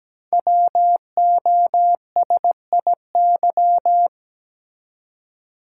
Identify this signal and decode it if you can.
Morse code, heard as a keyed tone, about 17 wpm, 700 Hz. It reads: WOSIY
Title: Code Groups - 5 Letter:Numbers - 17wpm